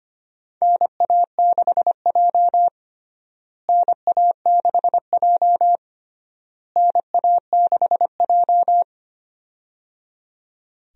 Morse code, heard as a keyed tone, about 25 wpm, 700 Hz. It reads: NA6J NA6J NA6J
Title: USA Call Signs - 4 Characters - Repeated 3x - 25wpm